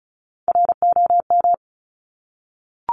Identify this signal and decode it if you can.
Morse code, heard as a keyed tone, about 36 wpm, 700 Hz. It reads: ROM
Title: Top Three Letter Combinations in Words 35wpm